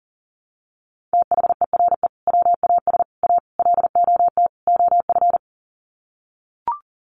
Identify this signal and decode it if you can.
Morse code, heard as a keyed tone, about 40 wpm, 700 Hz. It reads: THERE WAS A LOT OF
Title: Sets of 5 Words 40wpm